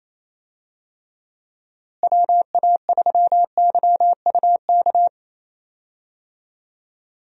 Morse code, heard as a keyed tone, about 27 wpm, 700 Hz. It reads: WA3YUK
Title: USA Call Signs - 6 Characters - Repeated 2x - 28wpm